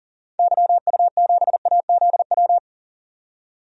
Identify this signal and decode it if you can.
Morse code, heard as a keyed tone, about 38 wpm, 700 Hz. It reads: YU7AZW
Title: DXCC-100 Call Signs - 6 Characters - 40wpm